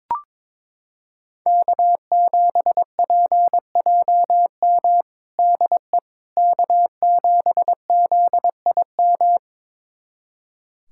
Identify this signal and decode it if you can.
Morse code, heard as a keyed tone, about 22 wpm, 700 Hz. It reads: K7PJM DE K7ZIM
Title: USA Callsign DE Callsign - 5 Characters - 22wpm